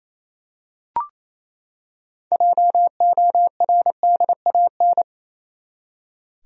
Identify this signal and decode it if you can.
Morse code, heard as a keyed tone, about 28 wpm, 700 Hz. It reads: JORDAN
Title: USA Common Names - 28wpm